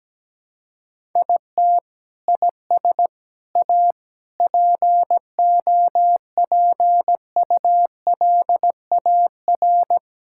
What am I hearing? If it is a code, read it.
Morse code, heard as a keyed tone, about 17 wpm, 700 Hz. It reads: IT IS A POPULAR
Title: Sentences from Top 1000 Words - Review Part 5 - 17wpm